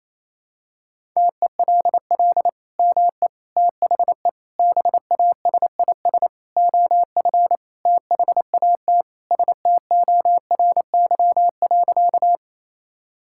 Morse code, heard as a keyed tone, about 28 wpm, 700 Hz. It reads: TELL ME THE BASIS OF THAT STORY.
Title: Sentences from Top 1000 Words - Review Part 5 - 28wpm